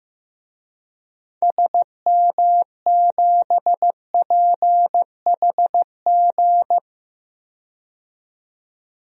Morse code, heard as a keyed tone, about 15 wpm, 700 Hz. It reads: SM7PHG
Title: DXCC-100 Call Signs - 6 Characters - 15wpm